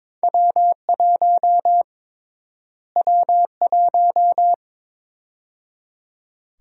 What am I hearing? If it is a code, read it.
Morse code, heard as a keyed tone, about 22 wpm, 700 Hz. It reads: W1 W1
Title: USA Call Sign Prefixes - Repeated 2x - 22wpm